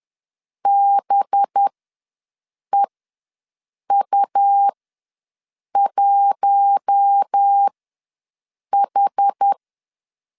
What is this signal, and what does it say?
Morse code, heard as a keyed tone, about 10 wpm, spaced as for 6 wpm, 800 Hz. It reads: BEU1H
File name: LEZIONI / 5 LBVC / 5 LB ws 10wpm 3616